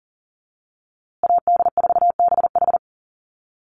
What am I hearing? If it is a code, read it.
Morse code, heard as a keyed tone, about 40 wpm, 700 Hz. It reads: AD4BH
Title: USA Call Signs Repeated 2x 40wpm